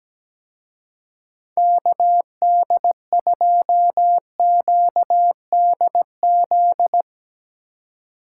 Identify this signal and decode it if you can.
Morse code, heard as a keyed tone, about 17 wpm, 700 Hz. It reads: KD2QDZ
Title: Speed Racing - USA 2x3 Call Signs 17wpm